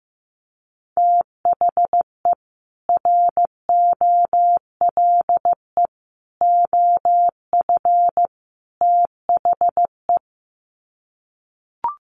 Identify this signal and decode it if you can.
Morse code, heard as a keyed tone, about 15 wpm, 700 Hz. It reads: THE ROLE OF THE
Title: Sets of 4 Words 15wpm